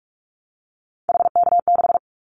Morse code, heard as a keyed tone, about 44 wpm, 700 Hz. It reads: HK6
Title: Callsign Prefixes for Top 100 DXCC - 45wpm